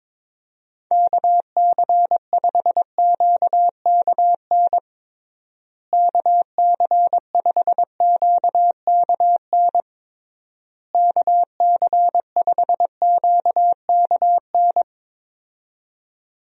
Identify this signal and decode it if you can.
Morse code, heard as a keyed tone, about 21 wpm, 700 Hz. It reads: KC5QKN KC5QKN KC5QKN
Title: USA Call Signs - 6 Characters - Repeated 3x - 22wpm